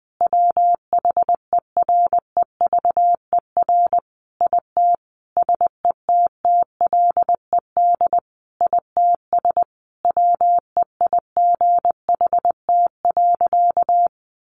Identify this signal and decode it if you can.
Morse code, heard as a keyed tone, about 20 wpm, 700 Hz. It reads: WHEREVER IT SETTLED ITS WEIGHT.
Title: A Sound Of Thunder 20wpm